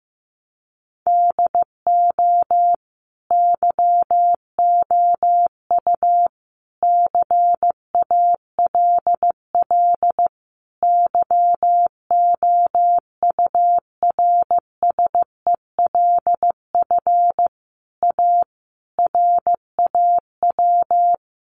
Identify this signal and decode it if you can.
Morse code, heard as a keyed tone, about 15 wpm, 700 Hz. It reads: DO YOU CALL YOURSELF A RAW
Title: Sentences from Top 1500 Words 15wpm